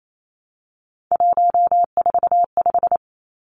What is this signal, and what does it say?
Morse code, heard as a keyed tone, about 28 wpm, 700 Hz. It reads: 145
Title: Numbers - 3-Digit 28wpm